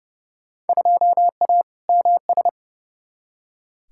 Morse code, heard as a keyed tone, about 30 wpm, 700 Hz. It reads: 2A MS
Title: Field Day Exchanges - 30wpm